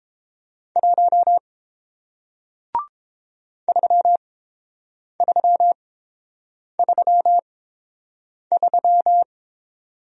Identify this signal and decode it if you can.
Morse code, heard as a keyed tone, about 30 wpm, spaced as for 7 wpm, 690 Hz. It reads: 1 3333